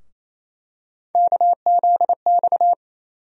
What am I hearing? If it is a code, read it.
Morse code, heard as a keyed tone, about 28 wpm, 700 Hz. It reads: KZX
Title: USA Call Sign Postfixes - 3 Letters 28wpm